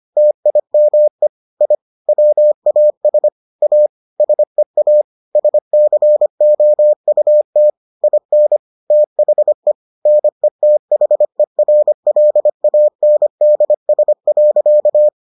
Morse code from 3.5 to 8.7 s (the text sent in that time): A SEA SCOUT IN